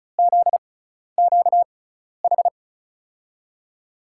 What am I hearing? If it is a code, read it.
Morse code, heard as a keyed tone, about 36 wpm, spaced as for 12 wpm, 690 Hz. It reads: ZQH
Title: USA Call Signs - 6 Characters - 6x Character Spacing - 35wpm